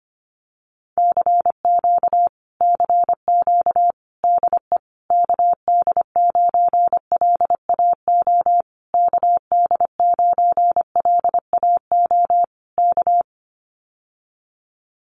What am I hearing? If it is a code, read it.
Morse code, heard as a keyed tone, about 25 wpm, 700 Hz. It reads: CQ CQ DE KD9LAO KD9LAO K
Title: Common QSO Phrases Encore 25wpm